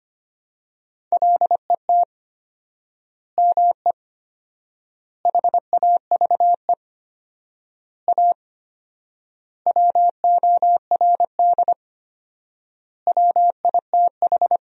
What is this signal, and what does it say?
Morse code, heard as a keyed tone, about 25 wpm, 700 Hz. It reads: LET ME HAVE A WORD WITH
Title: 4x Word-Spacing - Sentences from Top 300 Words - 25wpm